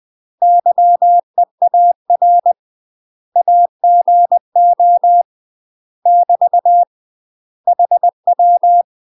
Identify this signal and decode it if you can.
Morse code, heard as a keyed tone, about 20 wpm, 700 Hz. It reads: YEAR AGO <BT> HW
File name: UK MASTER QSOs TRAINING / 20WPM / UK MASTER QSOs 0063